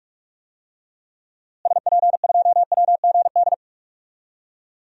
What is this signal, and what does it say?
Morse code, heard as a keyed tone, about 46 wpm, 700 Hz. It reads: SP2WGD
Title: DXCC-100 Call Signs - 6 Characters - Repeated 2x - 45wpm